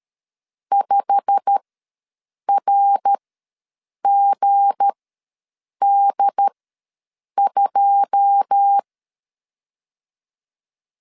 Morse code, heard as a keyed tone, about 13 wpm, spaced as for 7 wpm, 800 Hz. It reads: 5RGD2
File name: LEZIONI / 9 2 ws 12wpm 3406